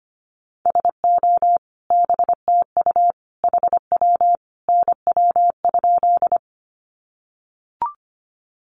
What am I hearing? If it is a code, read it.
Morse code, heard as a keyed tone, about 25 wpm, 700 Hz. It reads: SO BTU HW NW?
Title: Common QSO Phrases 25wpm